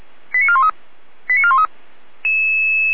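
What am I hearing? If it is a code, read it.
Five-tone selective call (ZVEI), heard as quick runs of five tones, tones of about 70 ms. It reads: 89413 89413
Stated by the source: Five-tone ZVEI call 89329 received on an SDR, from feuerwehr-orlamuende.org (Feuerwehr Orlamünde fire brigade).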